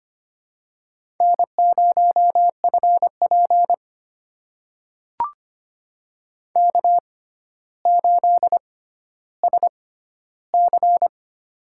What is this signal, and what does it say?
Morse code, heard as a keyed tone, about 25 wpm, spaced as for 13 wpm, 700 Hz. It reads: N0FP K8SC
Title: USA Call Signs - 4 Characters - 6x Character Spacing - 25wpm